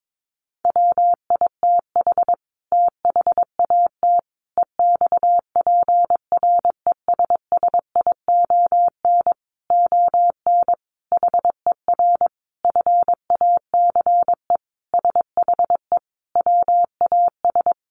Morse code, heard as a keyed tone, about 22 wpm, 700 Hz. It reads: WITH THAT EXPRESSION ON HER FACE SHE WAS
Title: Sentences from Top 1200 Words 22wpm